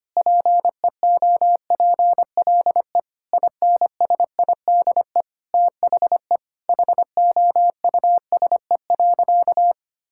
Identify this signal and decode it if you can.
Morse code, heard as a keyed tone, about 25 wpm, 700 Hz. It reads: PEOPLE INSIDE THE HOUSE.
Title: Sentences from Top 1000 Words - Review Part 6 - 25wpm